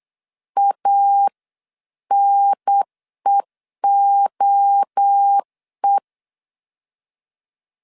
Morse code, heard as a keyed tone, about 8 wpm, 800 Hz. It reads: ANEOE